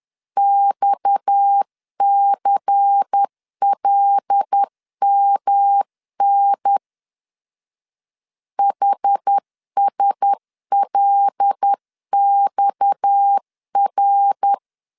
Morse code, heard as a keyed tone, about 10 wpm, 800 Hz. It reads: XCLMN HSLXR
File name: CW / DET / 1 6 1500C S C 10wpm 4252